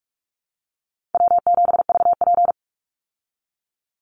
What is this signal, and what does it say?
Morse code, heard as a keyed tone, about 44 wpm, 700 Hz. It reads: W7VP
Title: Call Signs Repeated 2x - 45wpm